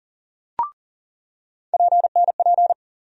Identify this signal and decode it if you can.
Morse code, heard as a keyed tone, about 40 wpm, 700 Hz. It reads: PNP
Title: USA Call Sign Postfixes - 3 Letters 40wpm